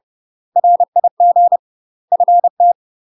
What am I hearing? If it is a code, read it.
Morse code, heard as a keyed tone, about 30 wpm, 700 Hz. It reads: RIG FT
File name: UK MASTER QSOs TRAINING / 30WPM / UK MASTER QSOs 0001